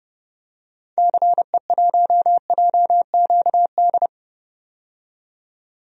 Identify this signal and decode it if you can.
Morse code, heard as a keyed tone, about 30 wpm, 700 Hz. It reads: CE1JQD
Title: DXCC-100 Call Signs - 6 Characters - 30wpm